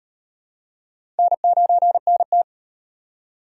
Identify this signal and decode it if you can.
Morse code, heard as a keyed tone, about 36 wpm, 700 Hz. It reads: N9NT